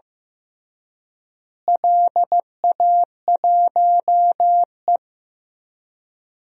Morse code, heard as a keyed tone, about 15 wpm, 700 Hz. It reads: LA1E
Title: DXCC-100 Call Signs - 4 Characters - Repeated 3x - 15wpm